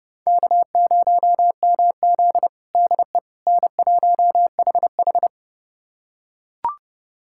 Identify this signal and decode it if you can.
Morse code, heard as a keyed tone, about 32 wpm, 700 Hz. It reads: K0MZ DE N1HH
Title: USA Callsign DE Callsign - 4 Characters - 30wpm